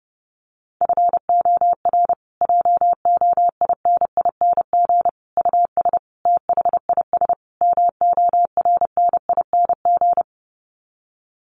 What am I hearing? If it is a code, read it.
Morse code, heard as a keyed tone, about 30 wpm, 700 Hz. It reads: FOR JOINING US THIS MORNING
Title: Sets of 5 Words Encore 30wpm